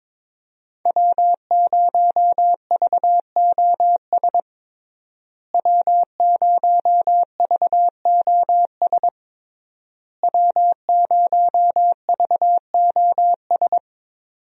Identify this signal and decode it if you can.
Morse code, heard as a keyed tone, about 22 wpm, 700 Hz. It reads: W0VOS W0VOS W0VOS
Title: USA Call Signs - 5 Characters - Repeated 3x - 22wpm